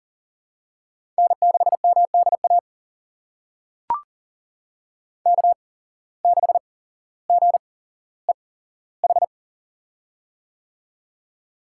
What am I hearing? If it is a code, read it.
Morse code, heard as a keyed tone, about 38 wpm, spaced as for 17 wpm, 700 Hz. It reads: N6MDA K6GEH
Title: USA Call Signs - 5 Characters - 8x Character Spacing - 40wpm